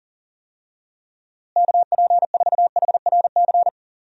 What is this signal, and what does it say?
Morse code, heard as a keyed tone, about 40 wpm, 700 Hz. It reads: KP4HRC
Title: USA Call Signs - 6 Characters - Repeated 2x - 40wpm